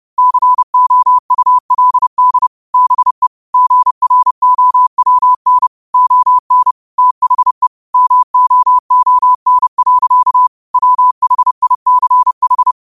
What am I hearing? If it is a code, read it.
Morse code, heard as a keyed tone, about 30 wpm, 1000 Hz. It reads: COULD BE GROWN ON THE MOON. WHICH
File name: qotdfiles / 30wpm